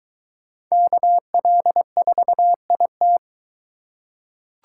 Morse code, heard as a keyed tone, about 22 wpm, 700 Hz. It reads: KL4IT